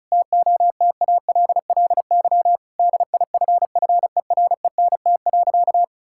Morse code, heard as a keyed tone, about 36 wpm, 700 Hz. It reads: TOTALLY DIFFERENT.
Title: Sentences from Top 1600 Words 35wpm